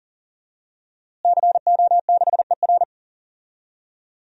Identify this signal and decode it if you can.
Morse code, heard as a keyed tone, about 40 wpm, 700 Hz. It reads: CO6ER
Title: DXCC-100 Call Signs - 5 Characters - Repeated 3x - 40wpm